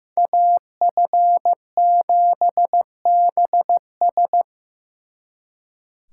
Morse code, heard as a keyed tone, about 18 wpm, 700 Hz. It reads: AF7BS
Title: Speed Racing - USA Call Signs 15wpm